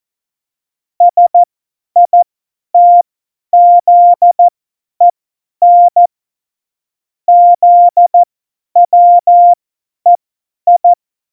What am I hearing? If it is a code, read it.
Morse code, heard as a keyed tone, about 14 wpm, spaced as for 10 wpm, 700 Hz. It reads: SITZEN ZWEI